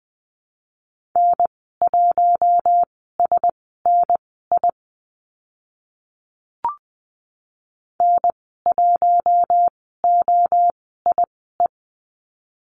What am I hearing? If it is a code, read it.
Morse code, heard as a keyed tone, about 20 wpm, spaced as for 14 wpm, 700 Hz. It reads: N1SNI N1OIE
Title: 042 Lesson 8g - Number 1 - Call Signs 20wpm